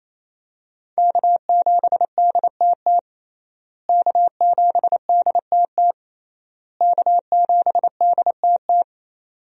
Morse code, heard as a keyed tone, about 28 wpm, 700 Hz. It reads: K7DTT K7DTT K7DTT
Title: USA Call Signs - 5 Characters - Repeated 3x - 28wpm